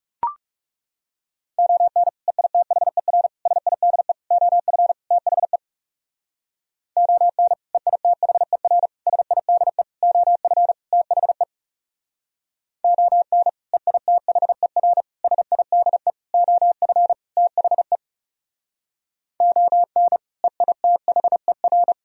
Morse code, heard as a keyed tone, about 36 wpm, 700 Hz. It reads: ON EITHER SIDE OF THE ON EITHER SIDE OF THE ON EITHER SIDE OF THE ON EITHER
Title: Speed-Racing - Sets of 5 Words 30wpm